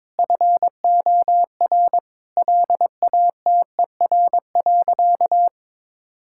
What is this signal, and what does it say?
Morse code, heard as a keyed tone, about 22 wpm, 700 Hz. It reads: FOR LATER.
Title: Sentences from Top 1000 Words - Review Part 6 - 22wpm